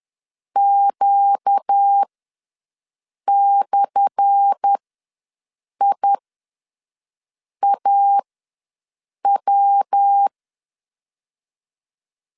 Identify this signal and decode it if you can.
Morse code, heard as a keyed tone, about 10 wpm, spaced as for 5 wpm, 800 Hz. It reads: Q/IAW
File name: LEZIONI / 10 SIGNS / 10D COMA ws 10wpm 4314